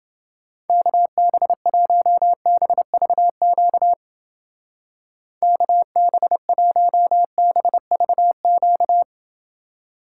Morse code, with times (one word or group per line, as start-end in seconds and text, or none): 0.7-3.9 KB1BVQ
5.4-9.0 KB1BVQ